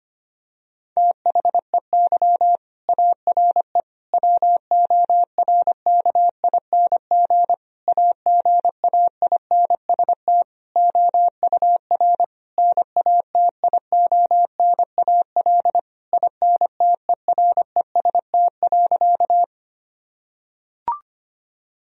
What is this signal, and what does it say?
Morse code, heard as a keyed tone, about 25 wpm, 700 Hz. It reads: THEY ARE WORKING AGAINST OUR NATIONAL INTEREST.
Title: Sentences from Top 1000 Words - Review Part 4 - 25wpm